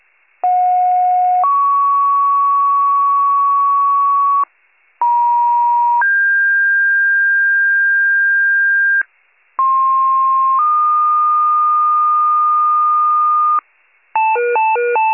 Two-tone page, heard as a one-second tone, then a three-second tone, about 730 then 1120 Hz, about 950 then 1600 Hz, about 1060 then 1220 Hz. It is heard from 0.4 to 13.6 s.
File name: pager / 83 RESCUE DUTY / wcalert